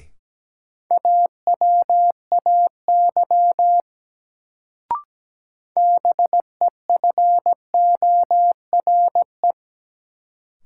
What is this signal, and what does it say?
Morse code, heard as a keyed tone, about 17 wpm, 700 Hz. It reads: AWAY BEFORE
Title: Sets of 2 Words - New Word Drill - 17wpm